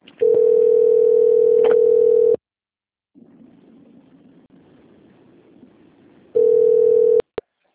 Ringback tone (North America), 440 + 480 Hz.